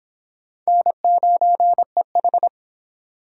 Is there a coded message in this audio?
Yes: Morse code from 0.7 to 2.5 s, heard as a keyed tone, about 31 wpm, 700 Hz. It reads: N9EH